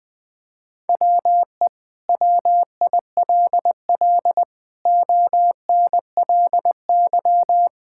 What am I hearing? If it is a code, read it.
Morse code, heard as a keyed tone, about 21 wpm, 700 Hz. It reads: WE WILL ONLY
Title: Sentences from Top 1000 Words - Review Part 5 - 20wpm